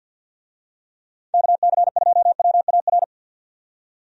Morse code, heard as a keyed tone, about 52 wpm, 700 Hz. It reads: KC2WAR
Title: USA Call Signs - 6 Characters - 50wpm